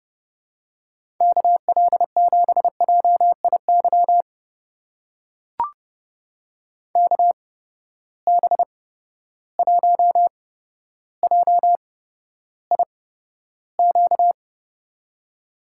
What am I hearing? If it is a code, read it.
Morse code, heard as a keyed tone, about 31 wpm, spaced as for 13 wpm, 700 Hz. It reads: KL7JIY KB1JIQ
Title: USA Call Signs - 6 Characters - 8x Character Spacing - 30wpm